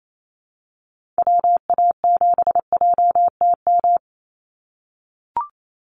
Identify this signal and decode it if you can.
Morse code, heard as a keyed tone, about 28 wpm, 700 Hz. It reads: WA7JTM
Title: NA SOTA Activator Callsigns 28wpm